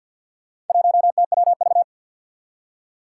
Morse code, heard as a keyed tone, about 47 wpm, 700 Hz. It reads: 1TWV
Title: Four Challenging Letter-Number Combinations 50wpm